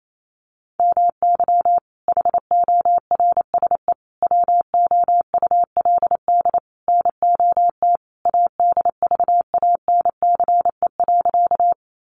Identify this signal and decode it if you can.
Morse code, heard as a keyed tone, about 28 wpm, 700 Hz. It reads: MY HORSE WOULD NOT ADVANCE.
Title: Sentences from Top 1400 Words 28wpm